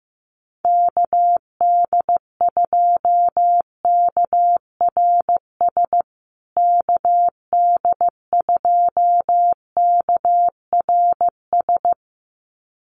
Morse code, heard as a keyed tone, about 15 wpm, 700 Hz. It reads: KD2KRS KD2KRS
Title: USA Call Signs Repeated 2x 15wpm